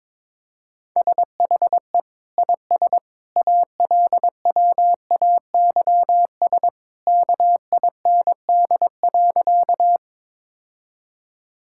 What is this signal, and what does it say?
Morse code, heard as a keyed tone, about 22 wpm, 700 Hz. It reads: SHE IS ALWAYS KIND.
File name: practice / 22wpm / Sentences with 2-Word Phrases - 22wpm